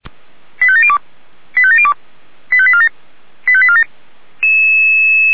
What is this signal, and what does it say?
Five-tone selective call (ZVEI), heard as quick runs of five tones, tones of about 70 ms. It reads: 85702 85702 85847 85847